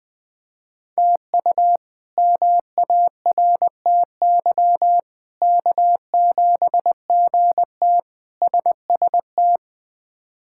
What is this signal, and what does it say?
Morse code, heard as a keyed tone, about 20 wpm, 700 Hz. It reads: TU MARTY K7GT SST
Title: K1USN SST QSO Practice - 20wpm